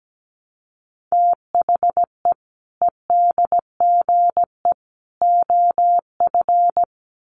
Morse code, heard as a keyed tone, about 17 wpm, 700 Hz. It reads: THE EDGE OF
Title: Sets of 3 Words 17wpm